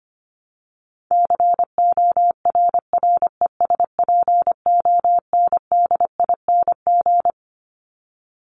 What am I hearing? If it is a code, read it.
Morse code, heard as a keyed tone, about 25 wpm, 700 Hz. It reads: CORRESPONDING
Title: ING Suffix - Top 100 Words with Base Words 25wpm